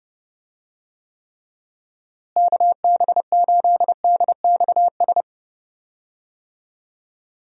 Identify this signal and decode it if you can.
Morse code, heard as a keyed tone, about 30 wpm, 700 Hz. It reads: KB8DXS
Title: USA Call Signs - 6 Characters - Repeated 2x - 30wpm